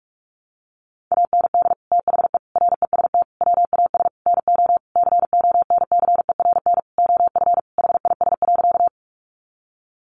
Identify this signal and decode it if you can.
Morse code, heard as a keyed tone, about 44 wpm, 700 Hz. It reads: AND THE REST WAS NO CONCERN OF HIS.
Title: Sentences from Top 1000 Words 45wpm